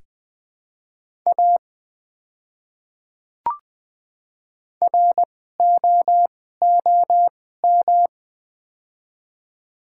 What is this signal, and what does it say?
Morse code, heard as a keyed tone, about 20 wpm, spaced as for 14 wpm, 700 Hz. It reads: A ROOM
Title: 2x Character Spacing - Top 500 Words - 20wpm